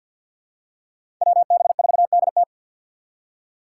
Morse code, heard as a keyed tone, about 52 wpm, 700 Hz. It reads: WB4DT
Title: USA Call Signs - 5 Characters - 8x Character Spacing - 50wpm